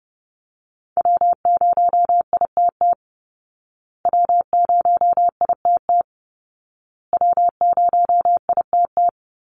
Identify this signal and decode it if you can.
Morse code, heard as a keyed tone, about 30 wpm, 700 Hz. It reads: W0ITT W0ITT W0ITT
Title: Call Signs Repeated 2x - 30wpm